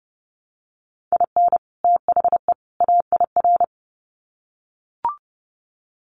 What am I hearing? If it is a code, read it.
Morse code, heard as a keyed tone, about 30 wpm, 700 Hz. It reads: IN THE AIR
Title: Sets of 3 Words 30wpm